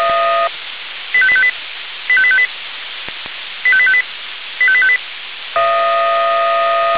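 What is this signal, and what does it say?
Five-tone selective call (ZVEI), heard as quick runs of five tones, tones of about 70 ms. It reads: 85858 85858 85858 85858
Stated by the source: ZVEI five-tone selective call 85310 received by SDR.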